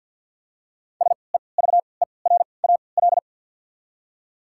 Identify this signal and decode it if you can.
Morse code, heard as a keyed tone, about 47 wpm, spaced as for 28 wpm, 700 Hz. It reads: SEVERAL
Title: 3x Character Spacing - Top 500 Words - 50wpm